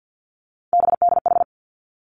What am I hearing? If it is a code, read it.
Morse code, heard as a keyed tone, about 50 wpm, 700 Hz. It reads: BDH
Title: Three Challenging Letter-Number Combinations 50wpm